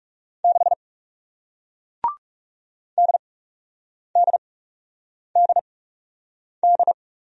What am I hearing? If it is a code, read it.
Morse code, heard as a keyed tone, about 41 wpm, spaced as for 8 wpm, 700 Hz. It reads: 6 DDDD